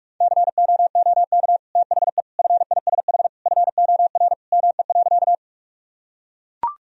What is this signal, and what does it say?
Morse code, heard as a keyed tone, about 44 wpm, 700 Hz. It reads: COOK THE FISH FOR ME.
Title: Sentences from Top 1000 Words - Review Part 4 - 45wpm